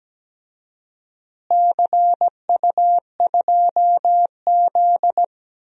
Morse code, heard as a keyed tone, about 17 wpm, 700 Hz. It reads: CU2Z
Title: DXCC-100 Call Signs - 4 Characters - Repeated 2x - 17wpm